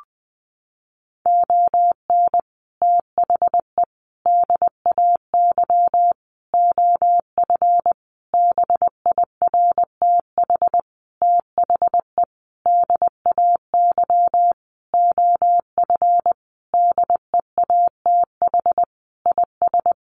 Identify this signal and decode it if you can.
Morse code, heard as a keyed tone, about 20 wpm, 700 Hz. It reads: ON THE DAY OF BIRTH THE DAY OF DEATH IS
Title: Sentences from Top 1100 Words 20wpm